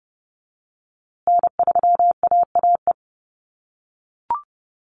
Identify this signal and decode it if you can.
Morse code, heard as a keyed tone, about 30 wpm, 700 Hz. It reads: N3AAE
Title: USA Call Signs Repeated 3x 30wpm